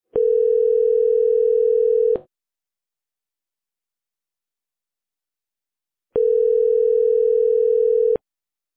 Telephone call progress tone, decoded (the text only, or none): ringback tone